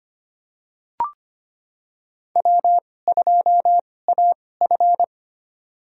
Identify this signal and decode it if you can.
Morse code, heard as a keyed tone, about 25 wpm, spaced as for 18 wpm, 700 Hz. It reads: W2AF
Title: USA Call Signs - 4 Characters - 2x Character Spacing - 25wpm